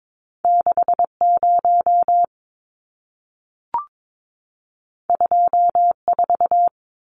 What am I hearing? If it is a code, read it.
Morse code, heard as a keyed tone, about 22 wpm, 700 Hz. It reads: 60 24
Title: Numbers - 2-Digit 22wpm